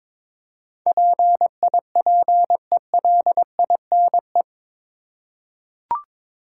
Morse code, heard as a keyed tone, about 21 wpm, 700 Hz. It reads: PIPELINE
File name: practice / 22wpm / Eight Letter Words - 22wpm